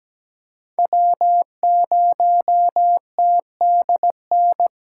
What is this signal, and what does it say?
Morse code, heard as a keyed tone, about 17 wpm, 700 Hz. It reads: W0TDN
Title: USA Call Signs - 5 Characters - 4x Character Spacing - 17wpm